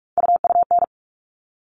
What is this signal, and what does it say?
Morse code, heard as a keyed tone, about 44 wpm, 700 Hz. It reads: UUN
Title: All Cut Numbers - 3-Digit 45wpm